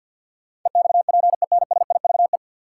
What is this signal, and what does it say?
Morse code, heard as a keyed tone, about 53 wpm, 700 Hz. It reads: EXPENSIVE
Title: Sentences from Top 1000 Words - Review Part 6 - New Word Drill - 50wpm